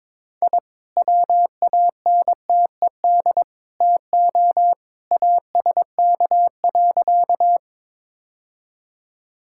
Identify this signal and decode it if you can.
Morse code, heard as a keyed tone, about 22 wpm, 700 Hz. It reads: I WANTED TO ASK.
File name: practice / 22wpm / Sentences with 3-Word Phrases - 22wpm